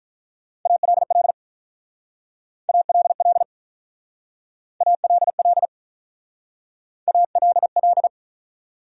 Morse code, heard as a keyed tone, about 44 wpm, 700 Hz. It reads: ALL ALL ALL ALL